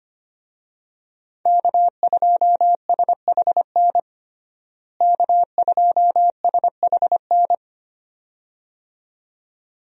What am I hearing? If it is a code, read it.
Morse code, heard as a keyed tone, about 25 wpm, 700 Hz. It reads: K2SHN K2SHN
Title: USA Call Signs - 5 Characters - Repeated 2x - 25wpm